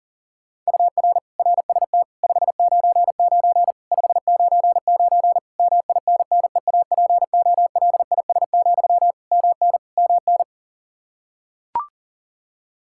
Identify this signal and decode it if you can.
Morse code, heard as a keyed tone, about 40 wpm, 700 Hz. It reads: UR RST 599 599 MINNEAPOLIS, MN MN
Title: Practice QSO - Exchange 1 - Beginning - 40wpm